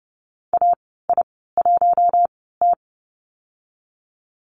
Morse code, heard as a keyed tone, about 28 wpm, spaced as for 17 wpm, 700 Hz. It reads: AI1T
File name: Lesson 15g - Number 5 - Call Signs 30wpm